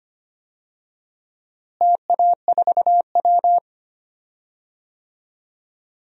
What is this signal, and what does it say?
Morse code, heard as a keyed tone, about 25 wpm, 700 Hz. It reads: TA4W
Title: DXCC-100 Call Signs - 4 Characters - Repeated 3x - 25wpm